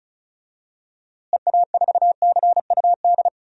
Morse code, heard as a keyed tone, about 34 wpm, 700 Hz. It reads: EA4CUD
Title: DXCC-100 Call Signs - 6 Characters - Repeated 2x - 35wpm